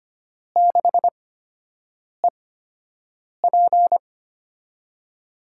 Morse code, heard as a keyed tone, about 25 wpm, spaced as for 7 wpm, 700 Hz. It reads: 6EP
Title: USA Call Signs - 4 Characters - 8x Character Spacing - 25wpm